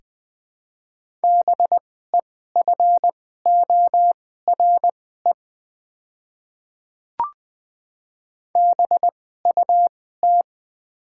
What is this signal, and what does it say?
Morse code, heard as a keyed tone, about 20 wpm, spaced as for 15 wpm, 700 Hz. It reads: BEFORE BUT
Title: 2x Character Spacing - Top 200 Words - 20wpm